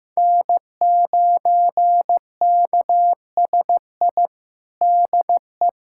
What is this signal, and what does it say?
Morse code, heard as a keyed tone, about 15 wpm, 700 Hz. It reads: N9KSI DE
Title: USA Callsign DE Callsign - 5 Characters - 15wpm